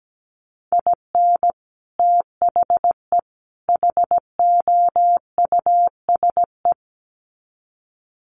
Morse code, heard as a keyed tone, about 17 wpm, 700 Hz. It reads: IN THE HOUSE
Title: Sets of 3 Words 17wpm